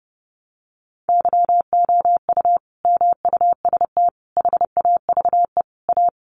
Morse code, heard as a keyed tone, about 30 wpm, 700 Hz. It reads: YOU MUST HAVE A
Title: Sentences from Top 500 Words 30wpm